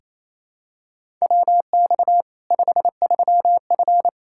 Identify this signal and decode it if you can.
Morse code, heard as a keyed tone, about 28 wpm, 700 Hz. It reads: WX 53F
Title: Practice QSO - Exchange 2 - Info - 28wpm